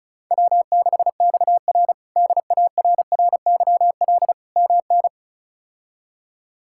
Morse code, heard as a keyed tone, about 34 wpm, 700 Hz. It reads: W6XR DARRYL MN
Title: K1USN SST QSO Practice - 35wpm